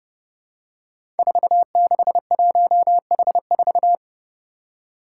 Morse code, heard as a keyed tone, about 28 wpm, 700 Hz. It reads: 461H4
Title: Five Challenging Letter-Number Combinations 30wpm